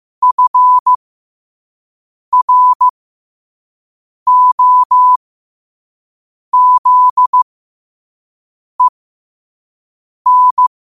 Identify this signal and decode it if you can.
Morse code, heard as a keyed tone, about 15 wpm, spaced as for 5 wpm, 1000 Hz. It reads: FROZEN
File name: 5wpm